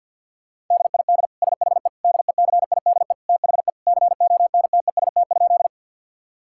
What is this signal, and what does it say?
Morse code, heard as a keyed tone, about 50 wpm, 700 Hz. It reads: DID SHE DECIDE THE CONTEST?
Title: Sentences from Top 1600 Words 50wpm